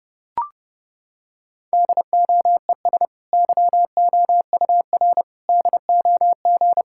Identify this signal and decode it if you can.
Morse code, heard as a keyed tone, about 29 wpm, 700 Hz. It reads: DOES YOUR DOG
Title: Sentences from Top 1600 Words 30wpm